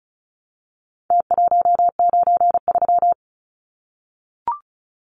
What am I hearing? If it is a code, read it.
Morse code, heard as a keyed tone, about 34 wpm, 700 Hz. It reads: T193
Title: Numbers - Contest Style - 4-Digit 35wpm